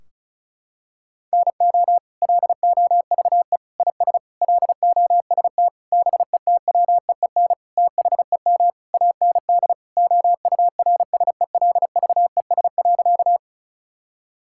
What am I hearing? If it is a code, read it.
Morse code, heard as a keyed tone, about 36 wpm, 700 Hz. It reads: NO LOVE IS LOST BETWEEN THEM AND OURSELVES.
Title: Sentences from Top 1600 Words 35wpm